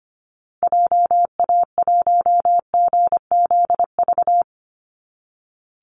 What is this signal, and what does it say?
Morse code, heard as a keyed tone, about 25 wpm, 700 Hz. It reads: JA1GZV
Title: Call Signs Repeated 2x - 25wpm